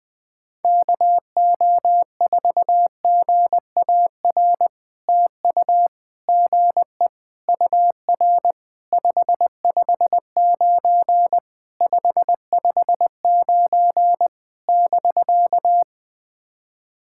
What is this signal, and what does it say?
Morse code, heard as a keyed tone, about 20 wpm, 700 Hz. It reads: KO4GAR TU GE UR 559 559 <BK>
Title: POTA QSO Practice - 20wpm